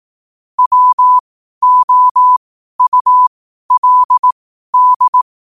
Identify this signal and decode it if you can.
Morse code, heard as a keyed tone, about 18 wpm, spaced as for 12 wpm, 1000 Hz. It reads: WOULD
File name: qotdfiles / 12wpm